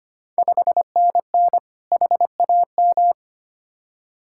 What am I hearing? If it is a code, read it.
Morse code, heard as a keyed tone, about 25 wpm, 700 Hz. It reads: 5NN HAM
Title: Ontario QSO Party Exchange 25wpm